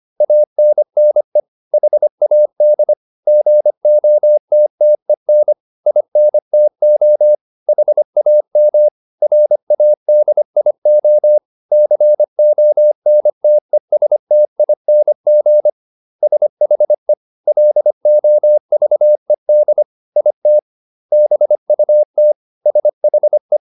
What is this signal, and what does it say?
Morse code, heard as a keyed tone, about 25 wpm, 600 Hz. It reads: ANNE HAD GOTTEN INTO HAM RADIO CONTESTING SHE LOVED IT BUT SHE